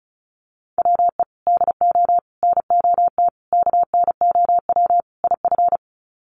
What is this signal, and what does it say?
Morse code, heard as a keyed tone, about 34 wpm, 700 Hz. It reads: WE DO NOT KNOW IF
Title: Sets of 5 Words Encore2 35wpm